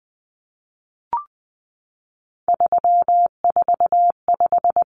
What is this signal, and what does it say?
Morse code, heard as a keyed tone, about 20 wpm, 700 Hz. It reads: 345
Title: Common Cut Numbers - 3-Digit 20wpm